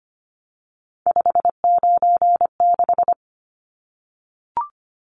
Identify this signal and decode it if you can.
Morse code, heard as a keyed tone, about 25 wpm, 690 Hz. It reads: 596
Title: Numbers - 3-Digit 25wpm